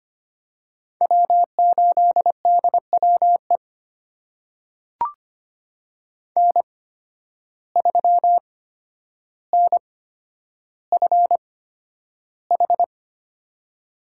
Morse code, heard as a keyed tone, about 25 wpm, spaced as for 11 wpm, 700 Hz. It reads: W8DWE N3NFH
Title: USA Call Signs - 5 Characters - 8x Character Spacing - 25wpm